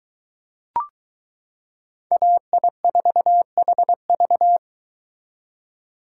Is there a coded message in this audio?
Yes: Morse code from 2.1 to 4.6 s, heard as a keyed tone, about 23 wpm, 700 Hz. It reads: AI4HV